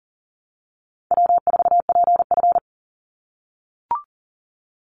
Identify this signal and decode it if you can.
Morse code, heard as a keyed tone, about 40 wpm, 700 Hz. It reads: W4PF
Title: Easy Call Signs - 40wpm